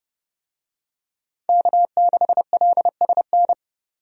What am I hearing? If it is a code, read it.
Morse code, heard as a keyed tone, about 30 wpm, 700 Hz. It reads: K6LSN
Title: USA Call Signs - 5 Characters - Repeated 2x - 30wpm